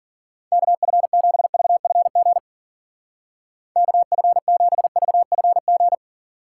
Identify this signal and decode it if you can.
Morse code, heard as a keyed tone, about 50 wpm, 700 Hz. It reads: KF7VFG KF7VFG
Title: Speed Racing - USA 2x3 Call Signs 40wpm